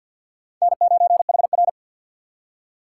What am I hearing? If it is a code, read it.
Morse code, heard as a keyed tone, about 62 wpm, 700 Hz. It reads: N9HR
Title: Speed Racing - USA 1x2 Call Signs 50wpm